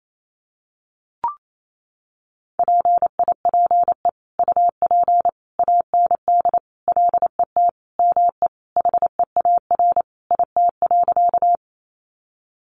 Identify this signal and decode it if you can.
Morse code, heard as a keyed tone, about 28 wpm, 700 Hz. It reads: PIPE UP AND LET ME HEAR IT.
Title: Sentences from Top 1500 Words 28wpm